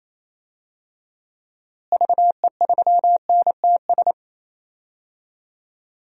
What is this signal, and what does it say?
Morse code, heard as a keyed tone, about 27 wpm, 700 Hz. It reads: VE3NTS
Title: DXCC-100 Call Signs - 6 Characters - Repeated 3x - 28wpm